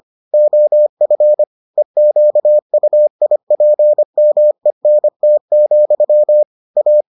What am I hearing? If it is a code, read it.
Morse code, heard as a keyed tone, about 25 wpm, 600 Hz. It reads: OF EQUIPMENT, A